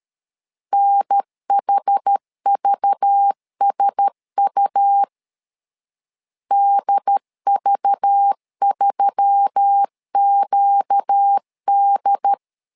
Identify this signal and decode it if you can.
Morse code, heard as a keyed tone, about 13 wpm, 800 Hz. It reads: NHVSU DV3QD
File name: LEZIONI / 5 LBVC / 5 LBV 12wpm 1855